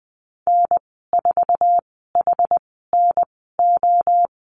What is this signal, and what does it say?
Morse code, heard as a keyed tone, about 20 wpm, spaced as for 15 wpm, 700 Hz. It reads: N4HNO
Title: 088 Lesson 15g - Number 5 - Call Signs 20wpm